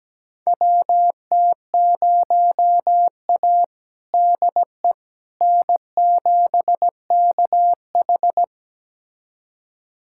Morse code, heard as a keyed tone, about 17 wpm, 700 Hz. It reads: WT0A DE N7KH
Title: USA Callsign DE Callsign - 4 Characters - 17wpm